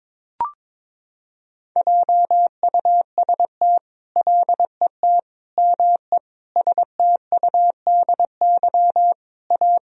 Morse code, heard as a keyed tone, about 22 wpm, 700 Hz. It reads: JUST LET ME STUDY A
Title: Sentences from Top 1000 Words - Review Part 1 - 22wpm